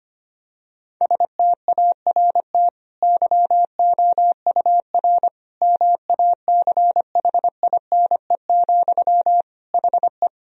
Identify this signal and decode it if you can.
Morse code, heard as a keyed tone, about 25 wpm, 700 Hz. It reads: START YOUR MACHINE, HE
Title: Sentences from Top 1000 Words - Review Part 2 - 25wpm